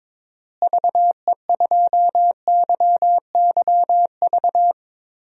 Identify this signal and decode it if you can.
Morse code, heard as a keyed tone, about 22 wpm, 700 Hz. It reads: VE2YYV
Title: DXCC-100 Call Signs - 6 Characters - Repeated 2x - 22wpm